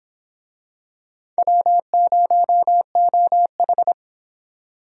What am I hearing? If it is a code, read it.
Morse code, heard as a keyed tone, about 26 wpm, 700 Hz. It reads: W0OH